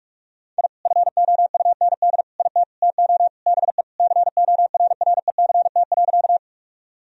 Morse code, heard as a keyed tone, about 41 wpm, 700 Hz. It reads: I FOUND IT TO BE CORRECT.
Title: Sentences from Top 1000 Words - Review Part 2 - 45wpm